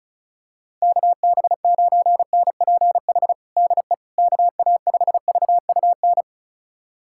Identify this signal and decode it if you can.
Morse code, heard as a keyed tone, about 36 wpm, 700 Hz. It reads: KB9NPH DE KA5VUN
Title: USA Callsign DE Callsign 35wpm